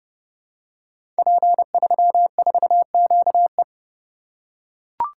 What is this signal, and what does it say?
Morse code, heard as a keyed tone, about 28 wpm, 700 Hz. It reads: P34QE
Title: Code Groups - 5 Letter:Numbers - 30wpm